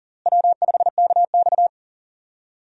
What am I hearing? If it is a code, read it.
Morse code, heard as a keyed tone, about 40 wpm, 700 Hz. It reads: W5KX
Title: USA Call Signs - 4 Characters - Repeated 3x - 40wpm